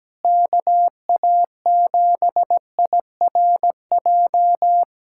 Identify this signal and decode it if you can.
Morse code, heard as a keyed tone, about 17 wpm, 700 Hz. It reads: KA7IRJ
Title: USA Callsign DE Callsign - 6 Characters - 17wpm